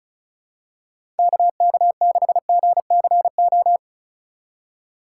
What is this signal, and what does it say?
Morse code, heard as a keyed tone, about 36 wpm, 700 Hz. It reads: KK6GCO
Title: USA Call Signs - 6 Characters - 2x Character Spacing - 35wpm